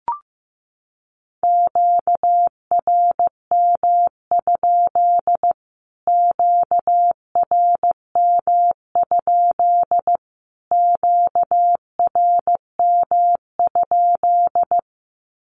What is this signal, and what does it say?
Morse code, heard as a keyed tone, about 15 wpm, 700 Hz. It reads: QRM? QRM? QRM?
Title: Q-Codes Repeated 3x 15wpm